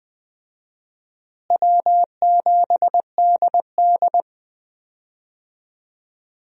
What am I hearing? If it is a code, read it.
Morse code, heard as a keyed tone, about 20 wpm, 700 Hz. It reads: W7DD
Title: USA Call Signs - 4 Characters - Repeated 3x - 20wpm